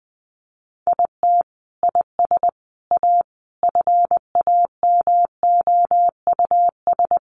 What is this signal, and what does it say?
Morse code, heard as a keyed tone, about 20 wpm, 700 Hz. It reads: IT IS A FAMOUS
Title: Sentences from Top 1500 Words 20wpm